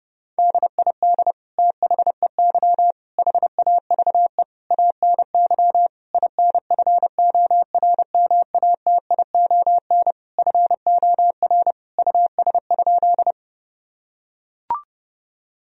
Morse code, heard as a keyed tone, about 30 wpm, 700 Hz. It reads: DID THEY HAVE ANY INFORMATION FOR US?
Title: Sentences from Top 1000 Words - Review Part 3 - 30wpm